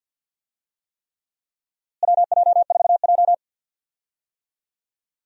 Morse code, heard as a keyed tone, about 51 wpm, 700 Hz. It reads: WJ4J
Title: USA Call Signs - 4 Characters - Repeated 3x - 50wpm